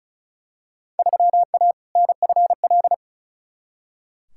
Morse code, heard as a keyed tone, about 34 wpm, 700 Hz. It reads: 3A NFL
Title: Field Day Exchanges - 35wpm